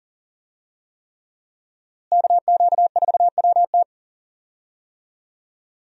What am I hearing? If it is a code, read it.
Morse code, heard as a keyed tone, about 40 wpm, 700 Hz. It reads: KQ4WT
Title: USA Call Signs - 5 Characters - Repeated 3x - 40wpm